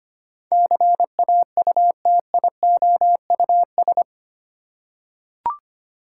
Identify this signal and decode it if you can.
Morse code, heard as a keyed tone, about 25 wpm, 700 Hz. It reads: CAUTIOUS
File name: 8 Letter Words - First 3 Letters Given - 25wpm